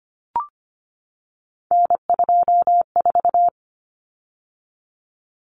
Morse code, heard as a keyed tone, about 25 wpm, 700 Hz. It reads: N24
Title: Common Cut Numbers - 3-Digit 25wpm